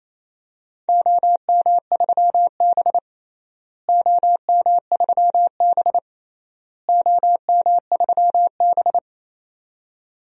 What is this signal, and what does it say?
Morse code, heard as a keyed tone, about 28 wpm, 700 Hz. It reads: OM3B OM3B OM3B
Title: DXCC-100 Call Signs - 4 Characters - Repeated 3x - 28wpm